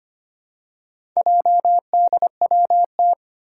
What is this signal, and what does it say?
Morse code, heard as a keyed tone, about 25 wpm, 700 Hz. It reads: JDWT
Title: Four Challenging Letter-Number Combinations 25wpm